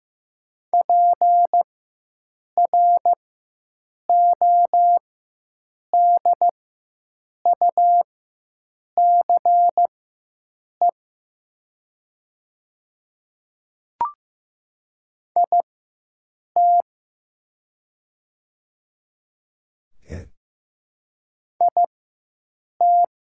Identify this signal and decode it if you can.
Morse code, heard as a keyed tone, about 15 wpm, spaced as for 7 wpm, 700 Hz. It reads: PRODUCE IT IT
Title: 4x Character Spacing - Top 500 Words - 15wpm